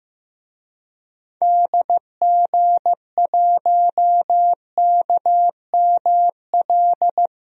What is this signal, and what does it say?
Morse code, heard as a keyed tone, about 15 wpm, 700 Hz. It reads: DG1KML
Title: DXCC-100 Call Signs - 6 Characters - 15wpm